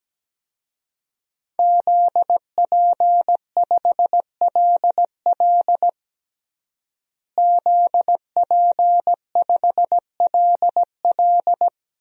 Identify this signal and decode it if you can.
Morse code, heard as a keyed tone, about 17 wpm, 700 Hz. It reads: ZP5LL ZP5LL
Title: DXCC-100 Call Signs - 5 Characters - Repeated 3x - 17wpm